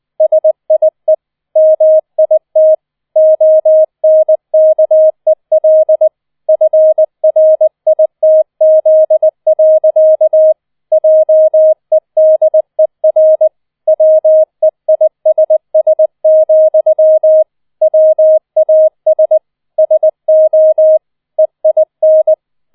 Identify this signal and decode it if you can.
Morse code, heard as a keyed tone, about 20 wpm, 630 Hz. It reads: SIE MIT ONKEL FRITZ. JEDER WEISS, WAS SO EIN
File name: 20wpm